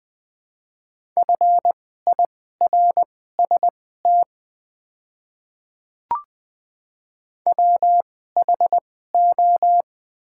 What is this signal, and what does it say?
Morse code, heard as a keyed tone, about 20 wpm, spaced as for 14 wpm, 700 Hz. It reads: FIRST WHO
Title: 2x Character Spacing - Top 100 Words - 20wpm